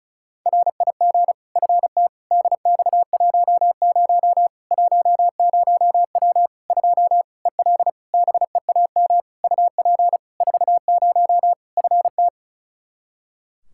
Morse code, heard as a keyed tone, about 36 wpm, 700 Hz. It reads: RIG FT DX10 10W 2 EL BEAM UP 40 FT
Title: Practice QSO - Exchange 2 - Info - 35wpm